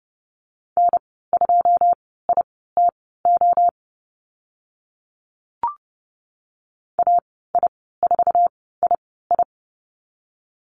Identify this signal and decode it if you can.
Morse code, heard as a keyed tone, about 30 wpm, spaced as for 17 wpm, 700 Hz. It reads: N2ITO AI4II
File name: Lesson 14g - Number 2 - Call Signs 30wpm